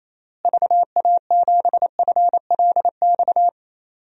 Morse code, heard as a keyed tone, about 28 wpm, 700 Hz. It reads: VA7FLX
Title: DXCC-100 Call Signs - 6 Characters - Repeated 2x - 28wpm